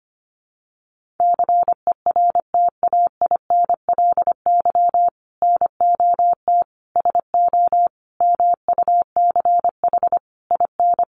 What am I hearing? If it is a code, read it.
Morse code, heard as a keyed tone, about 25 wpm, 700 Hz. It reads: CERTAINLY NOT SO MUCH IN
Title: Sentences from Top 1500 Words 25wpm